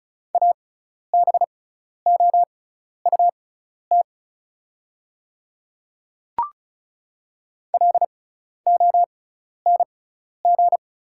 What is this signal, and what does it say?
Morse code, heard as a keyed tone, about 34 wpm, spaced as for 12 wpm, 690 Hz. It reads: ABOUT LONG
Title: 6x Character Spacing - Top 100 Words - 35wpm